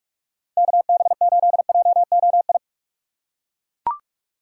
Morse code, heard as a keyed tone, about 42 wpm, 690 Hz. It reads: KB8JOI
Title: USA Call Signs - 6 Characters - 3x Character Spacing - 45wpm